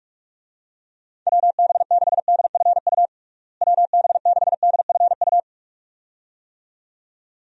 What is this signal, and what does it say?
Morse code, heard as a keyed tone, about 46 wpm, 700 Hz. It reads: WB6DFU WB6DFU
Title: USA Call Signs - 6 Characters - Repeated 2x - 45wpm